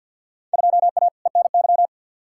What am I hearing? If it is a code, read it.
Morse code, heard as a keyed tone, about 51 wpm, 700 Hz. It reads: 2A ENY
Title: Field Day Exchanges Encore - 50wpm